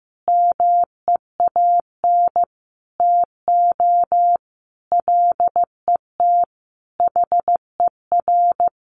Morse code, heard as a keyed tone, about 15 wpm, 700 Hz. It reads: MEAN TO LET HER
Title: Sentences from Top 1300 Words 15wpm